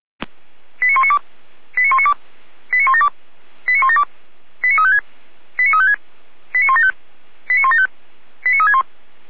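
Five-tone selective call (ZVEI), heard as quick runs of five tones, tones of about 70 ms. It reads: 89192 89192 89172 89172 89356 89356 89175 89175 89361
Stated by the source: ZVEI five-tone call 89340 received by SDR.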